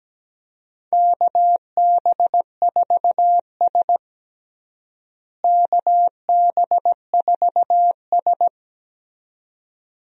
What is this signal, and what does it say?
Morse code, heard as a keyed tone, about 17 wpm, 700 Hz. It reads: KB4S KB4S
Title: USA Call Signs - 4 Characters - Repeated 2x - 17wpm